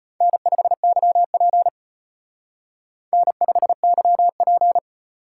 Morse code, heard as a keyed tone, about 32 wpm, 700 Hz. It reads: N5YP N5YP